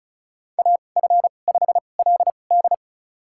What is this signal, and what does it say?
Morse code, heard as a keyed tone, about 34 wpm, spaced as for 25 wpm, 690 Hz. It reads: AF5LD
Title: USA Call Signs - 5 Characters - 2x Character Spacing - 35wpm